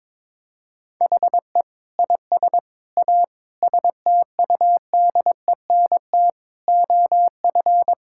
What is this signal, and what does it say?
Morse code, heard as a keyed tone, about 22 wpm, 700 Hz. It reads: HE IS A STUDENT OF
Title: Sentences from Top 1000 Words - Review Part 6 - 22wpm